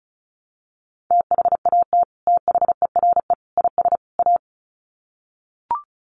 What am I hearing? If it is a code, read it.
Morse code, heard as a keyed tone, about 34 wpm, 700 Hz. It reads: THAT THERE IS A
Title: Sets of 4 Words 35wpm